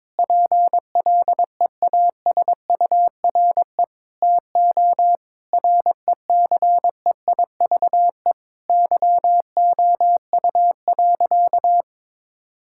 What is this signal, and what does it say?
Morse code, heard as a keyed tone, about 22 wpm, 700 Hz. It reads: PLEASURE TO RECEIVE YOU.
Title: Sentences from Top 1600 Words 22wpm